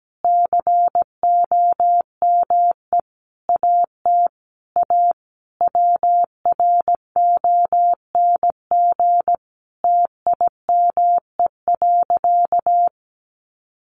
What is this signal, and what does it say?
Morse code, heard as a keyed tone, about 17 wpm, 700 Hz. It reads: COME AT A WRONG TIME.
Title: Sentences from Top 900 Words 17wpm